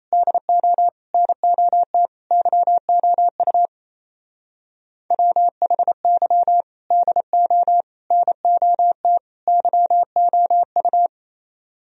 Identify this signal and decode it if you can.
Morse code, heard as a keyed tone, about 29 wpm, 700 Hz. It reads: DO NOT YOU WHY DO NOT YOU
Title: Speed-Racing - Sets of 4 Words 28wpm